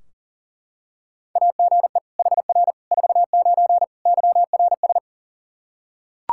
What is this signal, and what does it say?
Morse code, heard as a keyed tone, about 40 wpm, 700 Hz. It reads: AGE HR 49 YRS
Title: K1USN SST QSO Practice 40wpm